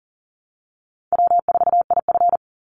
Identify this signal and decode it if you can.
Morse code, heard as a keyed tone, about 39 wpm, 700 Hz. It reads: W4IF
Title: Easy Call Signs - 40wpm